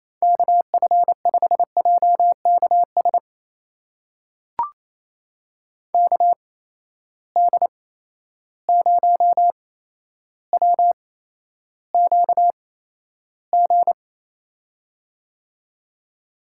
Morse code, heard as a keyed tone, about 28 wpm, spaced as for 12 wpm, 700 Hz. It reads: KF5JKS KD0WQG
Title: USA Call Signs - 6 Characters - 8x Character Spacing - 28wpm